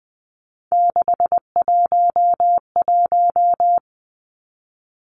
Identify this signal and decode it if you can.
Morse code, heard as a keyed tone, about 20 wpm, 700 Hz. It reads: 611
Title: Numbers - 3-Digit 20wpm